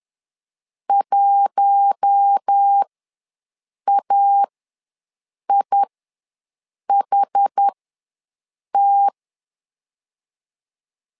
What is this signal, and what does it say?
Morse code, heard as a keyed tone, about 10 wpm, spaced as for 6 wpm, 800 Hz. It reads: 1AIHT
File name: LEZIONI / 4 Q1 ws 10wpm 2538